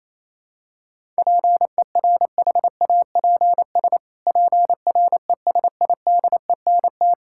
Morse code, heard as a keyed tone, about 28 wpm, 700 Hz. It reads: PERHAPS PRESIDENT
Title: Pangram Challenge v2 - 28wpm